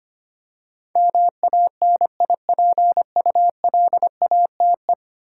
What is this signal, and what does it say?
Morse code, heard as a keyed tone, about 25 wpm, 700 Hz. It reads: MANIPULATE
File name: Ten Letter Words - 25wpm